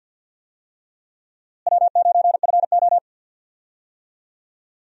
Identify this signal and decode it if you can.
Morse code, heard as a keyed tone, about 52 wpm, 700 Hz. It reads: W9FO
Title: USA Call Signs - 4 Characters - Repeated 3x - 50wpm